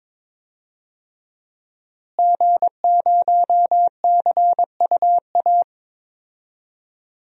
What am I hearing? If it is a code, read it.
Morse code, heard as a keyed tone, about 22 wpm, 700 Hz. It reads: G0CUA
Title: DXCC-100 Call Signs - 5 Characters - Repeated 3x - 22wpm